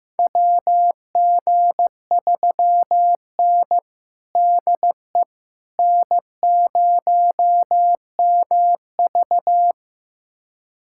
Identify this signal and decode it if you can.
Morse code, heard as a keyed tone, about 15 wpm, 700 Hz. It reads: WG3N DE N0MV
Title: USA Callsign DE Callsign - 4 Characters - 15wpm